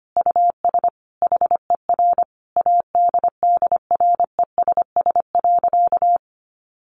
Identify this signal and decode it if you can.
Morse code, heard as a keyed tone, about 25 wpm, 700 Hz. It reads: US HER ADDRESS.
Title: Sentences from Top 1100 Words 25wpm